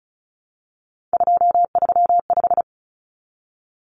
Morse code, heard as a keyed tone, about 34 wpm, 700 Hz. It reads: 235
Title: Common Cut Numbers - 3-Digit 35wpm